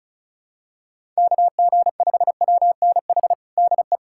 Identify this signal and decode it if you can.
Morse code, heard as a keyed tone, about 36 wpm, 700 Hz. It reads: KG5WNH DE
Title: USA Callsign DE Callsign 35wpm